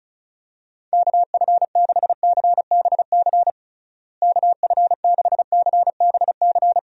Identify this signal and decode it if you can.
Morse code, heard as a keyed tone, about 36 wpm, 700 Hz. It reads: KF6CBC KF6CBC
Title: USA Call Signs - 6 Characters - Repeated 2x - 35wpm